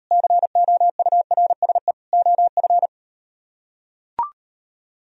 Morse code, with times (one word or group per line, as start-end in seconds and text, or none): 0.1-1.9 COURSE
2.1-2.9 OF